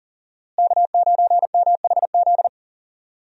Morse code, heard as a keyed tone, about 40 wpm, 700 Hz. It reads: K9MHZ
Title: USA Call Signs - 5 Characters - Repeated 2x - 40wpm